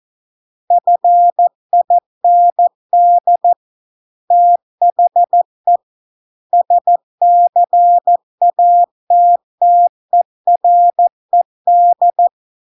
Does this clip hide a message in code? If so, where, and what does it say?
Yes: Morse code from 0.7 to 12.3 s, heard as a keyed tone, about 14 wpm, 700 Hz. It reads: FIND THE SCATTERED